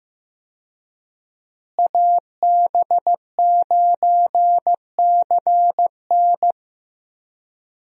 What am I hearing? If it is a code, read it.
Morse code, heard as a keyed tone, about 15 wpm, 700 Hz. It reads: AB9CN
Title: USA Call Signs - 5 Characters - Repeated 3x - 15wpm